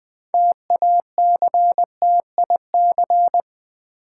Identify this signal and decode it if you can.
Morse code, heard as a keyed tone, about 20 wpm, 700 Hz. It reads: TACTIC
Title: Build A Word - 2 Letters At A Time - 8-Letter Words - 20wpm